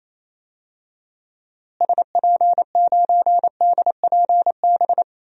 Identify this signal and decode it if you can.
Morse code, heard as a keyed tone, about 28 wpm, 700 Hz. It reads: SP9DPB
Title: DXCC-100 Call Signs - 6 Characters - Repeated 3x - 28wpm